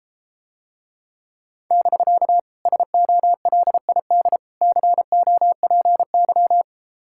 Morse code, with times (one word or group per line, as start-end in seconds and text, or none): 1.7-2.4 <BK>
2.7-4.4 SOLID
4.6-6.6 COPY